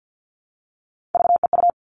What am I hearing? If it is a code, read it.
Morse code, heard as a keyed tone, about 50 wpm, 700 Hz. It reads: VEU
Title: All Cut Numbers - 3-Digit 50wpm